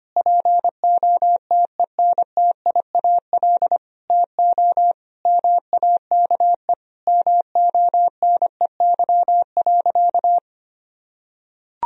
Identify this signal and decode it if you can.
Morse code, heard as a keyed tone, about 25 wpm, 700 Hz. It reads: POTENTIAL TO MAKE MONEY.
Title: Sentences from Top 1000 Words - Review Part 4 - 25wpm